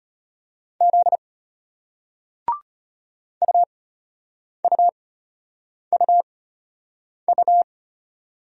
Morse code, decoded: Z UUUU

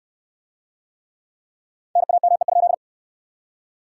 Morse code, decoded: NAME?

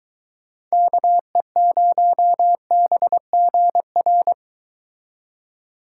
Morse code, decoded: KE0BGR

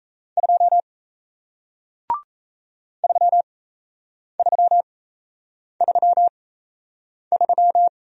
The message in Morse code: 2 3333

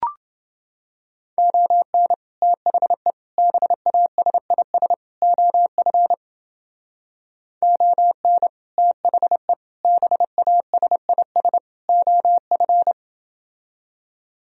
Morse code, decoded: ON THE BASIS OF ON THE BASIS OF

